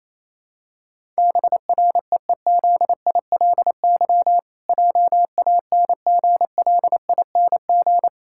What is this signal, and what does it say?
Morse code, heard as a keyed tone, about 28 wpm, 700 Hz. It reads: BREEZILY JANGLING
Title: Pangram Challenge v2 - 28wpm